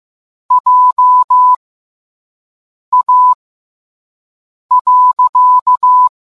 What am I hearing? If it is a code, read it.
Morse code, heard as a keyed tone, about 15 wpm, spaced as for 5 wpm, 1000 Hz. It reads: JA.